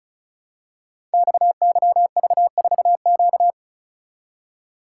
Morse code, heard as a keyed tone, about 36 wpm, 700 Hz. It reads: XYV4Q
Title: Code Groups - 5 Letter:Numbers - 35wpm